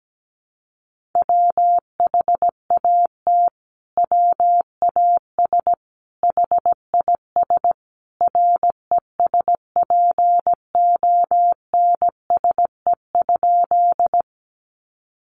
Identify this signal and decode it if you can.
Morse code, heard as a keyed tone, about 17 wpm, 700 Hz. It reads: WHAT WAS HIS RESPONSE?
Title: Sentences from Top 1400 Words 17wpm